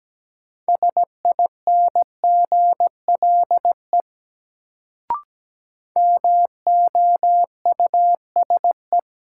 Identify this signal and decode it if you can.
Morse code, heard as a keyed tone, about 17 wpm, 700 Hz. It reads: SINGLE MOUSE
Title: Sentences from Top 700 Words - New Word Drill - 17wpm